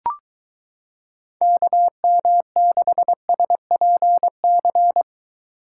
Morse code, heard as a keyed tone, about 22 wpm, 700 Hz. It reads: KM6SPC